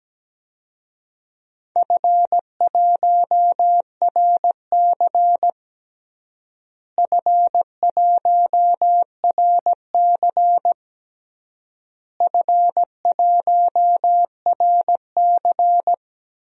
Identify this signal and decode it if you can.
Morse code, heard as a keyed tone, about 17 wpm, 700 Hz. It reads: F1RC F1RC F1RC
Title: DXCC-100 Call Signs - 4 Characters - Repeated 3x - 17wpm